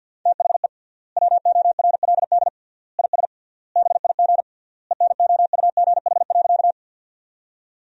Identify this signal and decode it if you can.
Morse code, heard as a keyed tone, about 50 wpm, 700 Hz. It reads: THE WORLD IS BIG ENOUGH.
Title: 3x Word-Spacing - Sentences from Top 400 Words - 50wpm